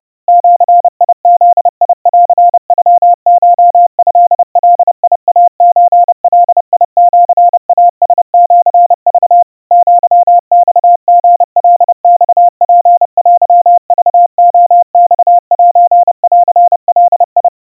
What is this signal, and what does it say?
Morse code (Japanese Wabun code), heard as a keyed tone, about 30 wpm, 700 Hz. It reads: ジブンノコトガイソガシイラシク アマリカマツテクレマセンガ
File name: kawa / Morse sample / kamyu 110 cpm 700 Hz Crispy